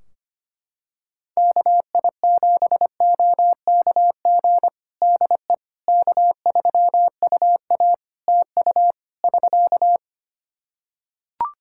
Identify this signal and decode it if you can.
Morse code, heard as a keyed tone, about 25 wpm, 700 Hz. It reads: KI7OKG DE K3UA TU <SK>
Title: Practice QSO - Exchange 4 - Ending - 25wpm